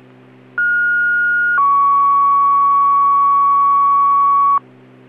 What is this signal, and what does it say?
Two-tone page, heard as a one-second tone, then a three-second tone, about 1430 then 1120 Hz. It is heard from 0.6 to 4.6 s.